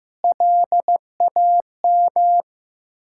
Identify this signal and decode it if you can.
Morse code, heard as a keyed tone, about 15 wpm, 700 Hz. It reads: LAM
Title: Ontario QSO Party Exchange 15wpm